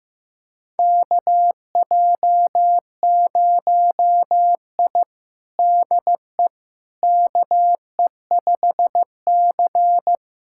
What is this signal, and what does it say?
Morse code, heard as a keyed tone, about 16 wpm, 700 Hz. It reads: KJ0I DE KE5C
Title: USA Callsign DE Callsign - 4 Characters - 15wpm